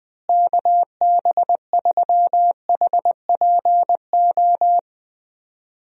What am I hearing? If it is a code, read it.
Morse code, heard as a keyed tone, about 20 wpm, 700 Hz. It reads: KB3HPO
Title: USA Call Signs - 6 Characters - Repeated 3x - 20wpm